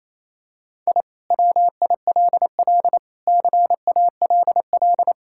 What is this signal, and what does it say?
Morse code, heard as a keyed tone, about 28 wpm, 700 Hz. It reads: I WILL CALL
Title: Sentences from Top 1000 Words - Review Part 6 - 28wpm